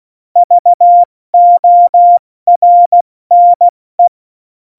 Morse code, heard as a keyed tone, about 16 wpm, spaced as for 14 wpm, 700 Hz. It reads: VORNE
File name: Morsekurs / Witze / Witz 21. August 2018, 70 F80